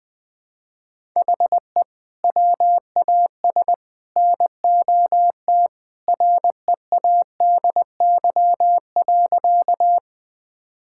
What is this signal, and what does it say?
Morse code, heard as a keyed tone, about 21 wpm, 700 Hz. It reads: HE WAS NOT READY.
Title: Sentences with 3-Word Phrases - Spoken Beginning - 20wpm